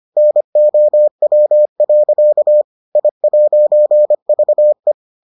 Morse code, heard as a keyed tone, about 25 wpm, 600 Hz. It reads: NOW. I'VE